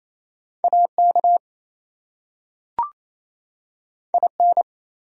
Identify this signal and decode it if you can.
Morse code, heard as a keyed tone, about 28 wpm, 700 Hz. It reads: AK IN
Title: US States and Canadian Provinces 28wpm